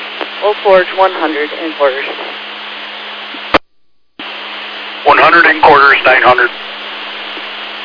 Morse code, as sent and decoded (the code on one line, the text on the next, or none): none